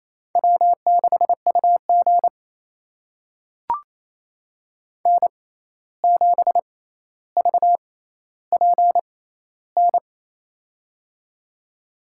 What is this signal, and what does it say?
Morse code, heard as a keyed tone, about 28 wpm, spaced as for 13 wpm, 700 Hz. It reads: W6UG N7VPN
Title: USA Call Signs - 6x Character Spacing - 28wpm